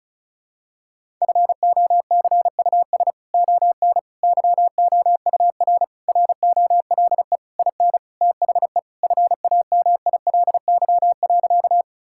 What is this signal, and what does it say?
Morse code, heard as a keyed tone, about 36 wpm, 700 Hz. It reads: FOCUS ON YOUR ROLE IN THE FAMILY.
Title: Sentences from Top 1000 Words - Review Part 6 - 35wpm